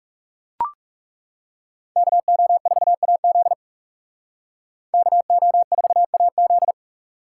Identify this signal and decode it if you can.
Morse code, heard as a keyed tone, about 36 wpm, 700 Hz. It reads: KO4AZ KO4AZ
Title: Speed Racing - USA Call Signs 30wpm